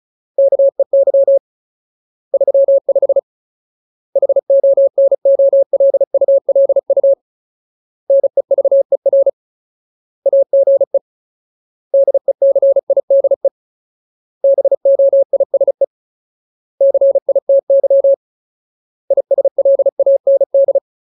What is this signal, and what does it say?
Morse code, heard as a keyed tone, about 34 wpm, 550 Hz. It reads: KEY 35 HONOLULU NEVER AGE DECIDE BOISE CITY ISLAND
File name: Ultimate Word List / 35WPM Ultimate